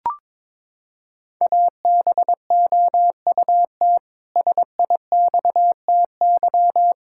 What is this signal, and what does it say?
Morse code, heard as a keyed tone, about 22 wpm, 700 Hz. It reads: ABOUT SIXTY
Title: Pangram Challenge v2 - 22wpm